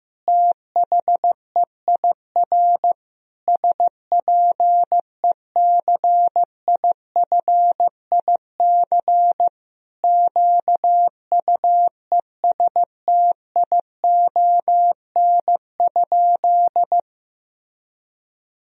Morse code, heard as a keyed tone, about 15 wpm, 700 Hz. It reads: THEIR SPECIFIC QUESTION?
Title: Sentences from Top 1000 Words - Review Part 3 - 15wpm